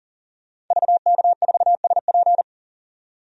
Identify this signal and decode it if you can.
Morse code, heard as a keyed tone, about 38 wpm, 700 Hz. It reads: VK4SP
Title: DXCC-100 Call Signs - 5 Characters - 40wpm